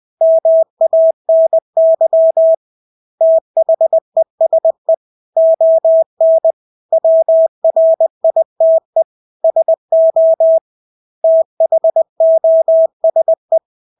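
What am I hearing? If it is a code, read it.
Morse code, heard as a keyed tone, about 20 wpm, 650 Hz. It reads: MANY THESE ON WRITE SO THOSE
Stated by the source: Morse practice of English words at 20 wpm.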